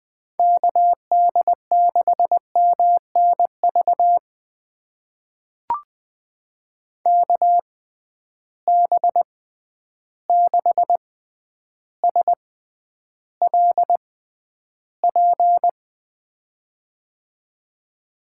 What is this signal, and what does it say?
Morse code, heard as a keyed tone, about 20 wpm, spaced as for 10 wpm, 700 Hz. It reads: KD6MNV KB6SLP
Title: USA Call Signs - 6 Characters - 6x Character Spacing - 20wpm